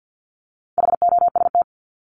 Morse code, heard as a keyed tone, about 50 wpm, 700 Hz. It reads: HOST
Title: Top 1501-1750 Words 50wpm